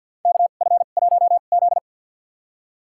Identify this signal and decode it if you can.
Morse code, heard as a keyed tone, about 52 wpm, spaced as for 37 wpm, 700 Hz. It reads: KF1Z
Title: USA Call Signs - 4 Characters - 2x Character Spacing - 50wpm